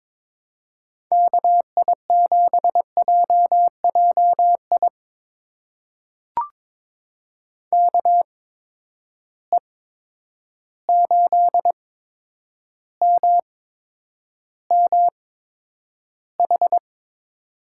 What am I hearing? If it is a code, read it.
Morse code, heard as a keyed tone, about 22 wpm, spaced as for 9 wpm, 700 Hz. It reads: KI7JJI KE8MMH
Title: USA Call Signs - 6 Characters - 8x Character Spacing - 22wpm